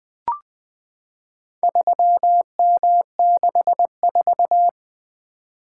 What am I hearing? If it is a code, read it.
Morse code, heard as a keyed tone, about 20 wpm, 700 Hz. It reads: 3M64
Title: Four Letter-Number Combinations - 20wpm